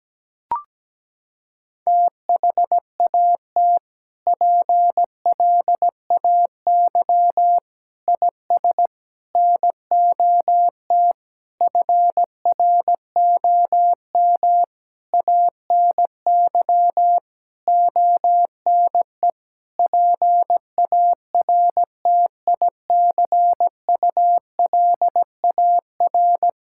Morse code, heard as a keyed tone, about 17 wpm, 700 Hz. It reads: THAT PLAY IS NOT FROM ANY ONE PARTICULAR
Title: Sentences from Top 1000 Words - Review Part 5 - 17wpm